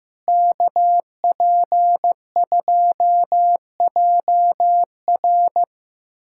Morse code, heard as a keyed tone, about 15 wpm, 700 Hz. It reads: KP2JR
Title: USA Callsign DE Callsign - 5 Characters - 15wpm